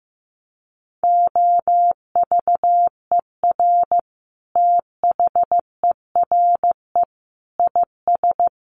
Morse code, heard as a keyed tone, about 15 wpm, 700 Hz. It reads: OVER THERE IS
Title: Sentences from Top 1200 Words 15wpm